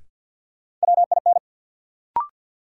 Morse code, heard as a keyed tone, about 52 wpm, 700 Hz. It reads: WIN
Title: Sentences from Top 1500 Words - New Word Drill - 50wpm